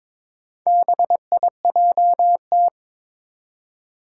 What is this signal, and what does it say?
Morse code, heard as a keyed tone, about 22 wpm, 700 Hz. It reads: BIJT
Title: Four Challenging Letter-Number Combinations 22wpm